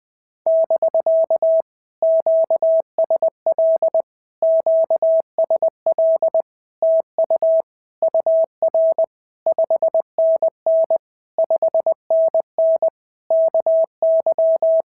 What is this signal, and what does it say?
Morse code, heard as a keyed tone, about 21 wpm, 650 Hz. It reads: <BK> QSL QSL TU UR 5NN 5NN KY
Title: POTA QSO Practice - 20wpm